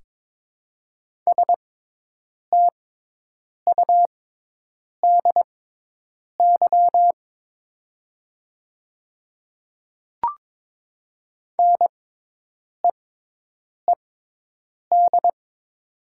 Morse code, heard as a keyed tone, about 22 wpm, spaced as for 8 wpm, 700 Hz. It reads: STUDY NEED
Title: 6x Character Spacing - Top 500 Words - 22wpm